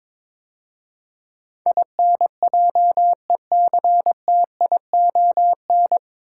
Morse code, heard as a keyed tone, about 22 wpm, 700 Hz. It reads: INJECTION
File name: 9 Letter Words - First 3 Letters Given - 22wpm